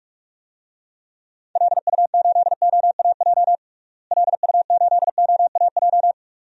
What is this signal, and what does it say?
Morse code, heard as a keyed tone, about 44 wpm, 700 Hz. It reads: LU8OAJ LU8OAJ
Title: DXCC-100 Call Signs - 6 Characters - Repeated 2x - 45wpm